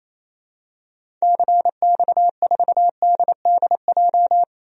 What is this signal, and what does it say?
Morse code, heard as a keyed tone, about 28 wpm, 700 Hz. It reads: CX4DDJ
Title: DXCC-100 Call Signs - 6 Characters - 28wpm